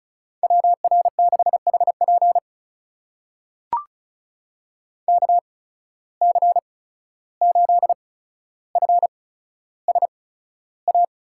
Morse code, heard as a keyed tone, about 36 wpm, spaced as for 14 wpm, 700 Hz. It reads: WR6HP KC8FSA
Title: USA Call Signs - 8x Character Spacing - 35wpm